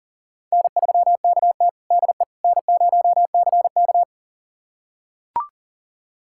Morse code, heard as a keyed tone, about 40 wpm, 700 Hz. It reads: N3KT DE N0CK
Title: USA Callsign DE Callsign - 4 Characters - 40wpm